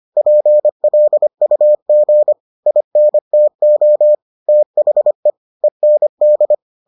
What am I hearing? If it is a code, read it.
Morse code, heard as a keyed tone, about 25 wpm, 600 Hz. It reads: PLUG INTO THE END